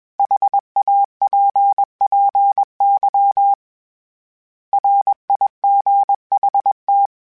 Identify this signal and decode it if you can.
Morse code, heard as a keyed tone, about 21 wpm, 800 Hz. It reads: HAPPY RIGHT